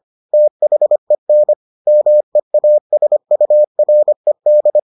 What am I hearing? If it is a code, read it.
Morse code, heard as a keyed tone, about 25 wpm, 600 Hz. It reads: THEN MEASURED